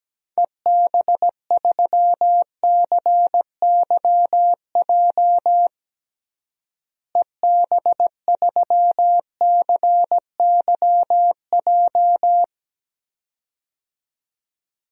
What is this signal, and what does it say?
Morse code, heard as a keyed tone, about 17 wpm, 700 Hz. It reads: EB3CYJ EB3CYJ
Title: DXCC-100 Call Signs - 6 Characters - Repeated 2x - 17wpm